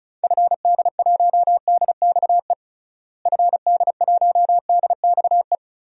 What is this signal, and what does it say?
Morse code, heard as a keyed tone, about 36 wpm, 700 Hz. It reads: FD1DXE FD1DXE
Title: DXCC-100 Call Signs - 6 Characters - Repeated 3x - 35wpm